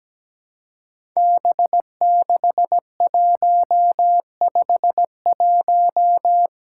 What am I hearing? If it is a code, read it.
Morse code, heard as a keyed tone, about 17 wpm, 700 Hz. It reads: B6151
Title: Five Challenging Letter-Number Combinations 17wpm